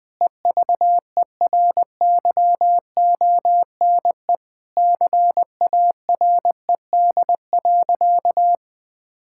Morse code, heard as a keyed tone, about 20 wpm, 700 Hz. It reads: EVERYONE CARED.
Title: Sentences with 8-Word Phrases - 20wpm